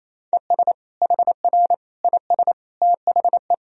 Morse code, heard as a keyed tone, about 28 wpm, 700 Hz. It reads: ES HR IS THE
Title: Sentences from Top 500 Words and Abbreviations 28wpm